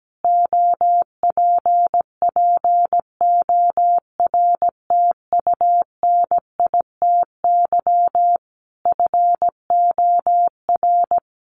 Morse code, heard as a keyed tone, about 17 wpm, 700 Hz. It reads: OPPORTUNITY FOR
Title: Sentences from Top 1400 Words 17wpm